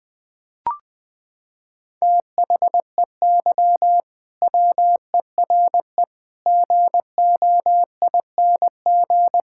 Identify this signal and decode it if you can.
Morse code, heard as a keyed tone, about 20 wpm, 700 Hz. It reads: THEY WERE GOING
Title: Sentences with 3-Word Phrases - 20wpm